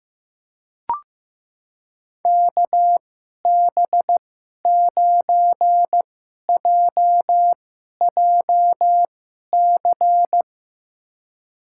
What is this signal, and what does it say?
Morse code, heard as a keyed tone, about 15 wpm, spaced as for 11 wpm, 700 Hz. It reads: KB9JJC
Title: USA Call Signs - 2x Character Spacing - 15wpm